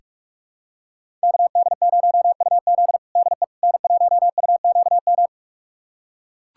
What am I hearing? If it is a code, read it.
Morse code, heard as a keyed tone, about 41 wpm, 700 Hz. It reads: KD0UZ DE N1UQM
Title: USA Callsign DE Callsign - 5 Characters - 45wpm